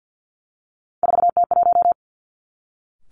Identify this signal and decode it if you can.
Morse code, heard as a keyed tone, about 50 wpm, 700 Hz. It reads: 4T1
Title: Three Challenging Letter-Number Combinations 50wpm